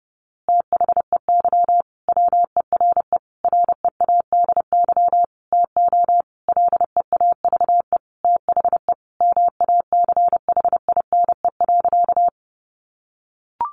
Morse code, heard as a keyed tone, about 28 wpm, 700 Hz. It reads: THEY WERE READY TO LEAVE THE MACHINE.
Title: A Sound Of Thunder 30wpm